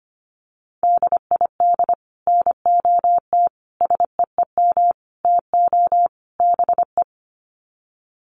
Morse code, heard as a keyed tone, about 25 wpm, 700 Hz. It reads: DID NOT SEEM TO BE
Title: Sets of 5 Words Encore 25wpm